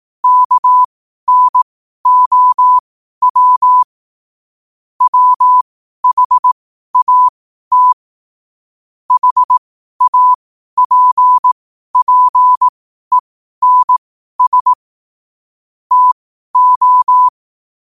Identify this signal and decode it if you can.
Morse code, heard as a keyed tone, about 18 wpm, spaced as for 12 wpm, 1000 Hz. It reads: KNOW WHAT HAPPENS TO
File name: qotdfiles / 12wpm